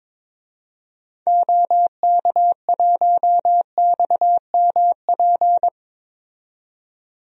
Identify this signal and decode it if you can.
Morse code, heard as a keyed tone, about 22 wpm, 700 Hz. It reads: OK1XMP
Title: DXCC-100 Call Signs - 6 Characters - 22wpm